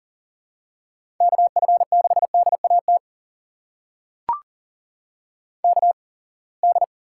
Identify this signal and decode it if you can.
Morse code, heard as a keyed tone, about 40 wpm, spaced as for 28 wpm, 700 Hz. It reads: KF6DAT KD
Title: USA Call Signs - 6 Characters - 8x Character Spacing - 40wpm